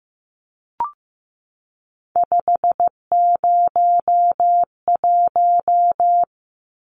Morse code, heard as a keyed tone, about 15 wpm, 700 Hz. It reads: 501
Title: Numbers - 3-Digit 15wpm